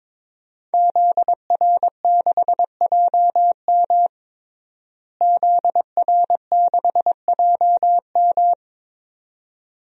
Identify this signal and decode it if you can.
Morse code, heard as a keyed tone, about 22 wpm, 700 Hz. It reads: ZR6JM ZR6JM
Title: DXCC-100 Call Signs - 5 Characters - Repeated 2x - 22wpm